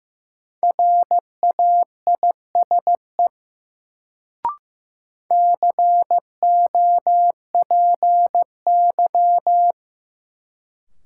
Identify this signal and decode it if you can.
Morse code, heard as a keyed tone, about 15 wpm, 700 Hz. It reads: RAISE COPY
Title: Sentences from Top 1400 Words - New Word Drill - 15wpm